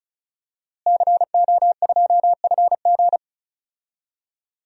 Morse code, heard as a keyed tone, about 36 wpm, 700 Hz. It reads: CO2FG
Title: DXCC-100 Call Signs - 5 Characters - Repeated 2x - 35wpm